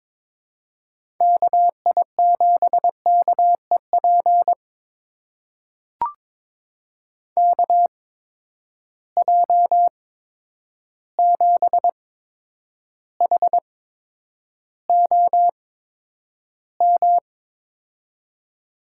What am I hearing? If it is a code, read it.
Morse code, heard as a keyed tone, about 21 wpm, spaced as for 9 wpm, 700 Hz. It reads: KI7KEP KJ7HOM
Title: USA Call Signs - 6 Characters - 8x Character Spacing - 22wpm